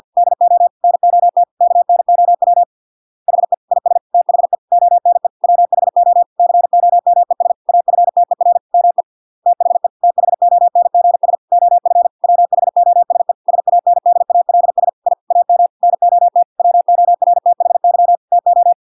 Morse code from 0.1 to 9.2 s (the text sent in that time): DO NOT KNOW HE IS THE ONE WHO COMES AFTER ME